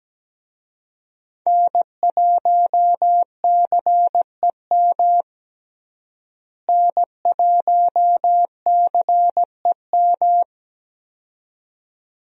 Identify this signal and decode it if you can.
Morse code, heard as a keyed tone, about 17 wpm, 700 Hz. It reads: N1CEM N1CEM
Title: USA Call Signs - 5 Characters - Repeated 2x - 17wpm